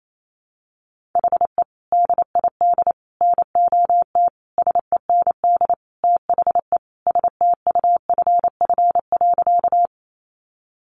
Morse code, heard as a keyed tone, about 28 wpm, 700 Hz. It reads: HE DID NOT SEND THE STUFF.
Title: Sentences from Top 1500 Words 28wpm